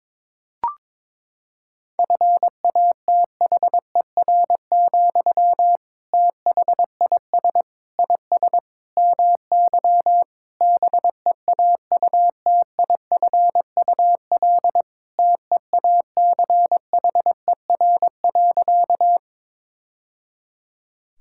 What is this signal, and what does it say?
Morse code, heard as a keyed tone, about 22 wpm, 700 Hz. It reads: FATHER, THIS IS MY BEAUTIFUL TEACHER.
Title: Sentences from Top 1000 Words - Review Part 6 - 22wpm